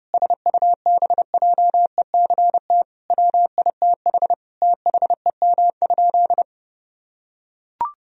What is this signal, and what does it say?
Morse code, heard as a keyed tone, about 32 wpm, 700 Hz. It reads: SUBJECT WITH THEM?
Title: Sentences from Top 1000 Words - Review Part 1 - 30wpm